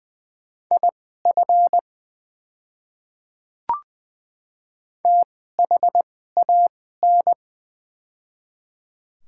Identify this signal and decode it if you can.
Morse code, heard as a keyed tone, about 20 wpm, spaced as for 15 wpm, 700 Hz. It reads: IF THAN
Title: 2x Character Spacing - Top 100 Words - 20wpm